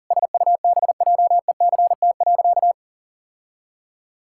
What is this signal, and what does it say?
Morse code, heard as a keyed tone, about 37 wpm, 700 Hz. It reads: SUBJECT.
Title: Sentences from Top 1000 Words - Review Part 6 - 40wpm